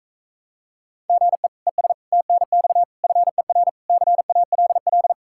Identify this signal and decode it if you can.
Morse code, heard as a keyed tone, about 45 wpm, 700 Hz. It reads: GE ES TNX FER CALL